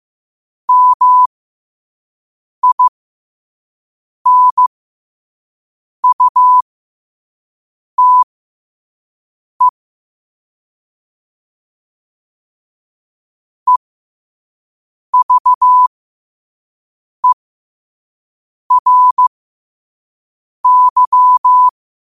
Morse code, heard as a keyed tone, about 15 wpm, spaced as for 5 wpm, 1000 Hz. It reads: MINUTE EVERY
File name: qotdfiles / 5wpm